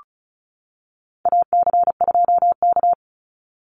Morse code, heard as a keyed tone, about 34 wpm, 700 Hz. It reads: AC2K
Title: Call Signs - 35wpm